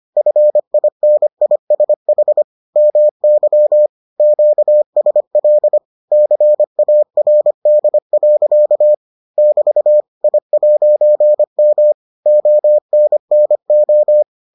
Morse code, heard as a keyed tone, about 25 wpm, 600 Hz. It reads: FINISH MY QSL CARD. <BT> I'M ONNO